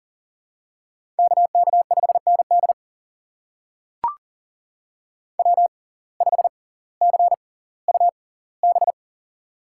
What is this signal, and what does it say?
Morse code, heard as a keyed tone, about 40 wpm, spaced as for 20 wpm, 700 Hz. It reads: KK5ND W5CUB
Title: USA Call Signs - 5 Characters - 6x Character Spacing - 40wpm